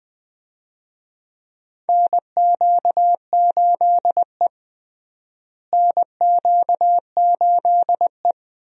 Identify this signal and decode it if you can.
Morse code, heard as a keyed tone, about 20 wpm, 700 Hz. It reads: NQ8E NQ8E
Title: USA Call Signs - 4 Characters - Repeated 2x - 20wpm